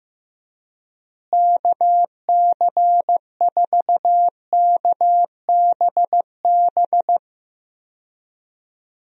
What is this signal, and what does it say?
Morse code, heard as a keyed tone, about 15 wpm, 700 Hz. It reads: KC4KBB
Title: USA Call Signs - 6 Characters - Repeated 3x - 15wpm